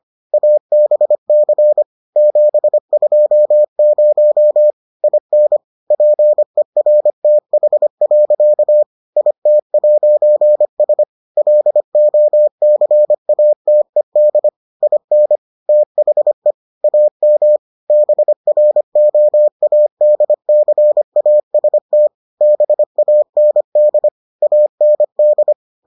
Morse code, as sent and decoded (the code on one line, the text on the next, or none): .- -... -.-. / --... ..--- ----- / .. -. / .--. .-. - .... .-.-.- / .. - .----. ... / .-.. --- -.-. .- - . -.. / .. -. / - .... / .- -- / -... .-. --- .- -.. -.-. .- ... - / -... .- -. -.. / .- -. -..
ABC 720 IN PERTH. IT'S LOCATED IN THE AM BROADCAST BAND AND